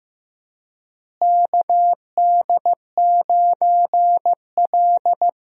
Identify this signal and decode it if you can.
Morse code, heard as a keyed tone, about 16 wpm, 700 Hz. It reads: KD9L
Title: USA Call Signs - 4 Characters - Repeated 3x - 15wpm